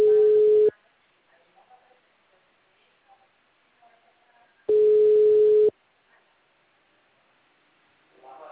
Ringback tone (Europe, 425 Hz).